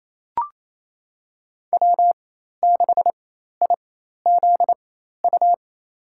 Morse code, heard as a keyed tone, about 28 wpm, spaced as for 13 wpm, 700 Hz. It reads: W6IZU
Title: USA Call Signs - 4x Character Spacing - 28wpm